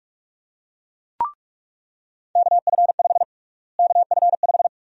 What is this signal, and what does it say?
Morse code, heard as a keyed tone, about 41 wpm, 700 Hz. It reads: KF5 KF5
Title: USA Call Sign Prefixes - Repeated 2x - 45wpm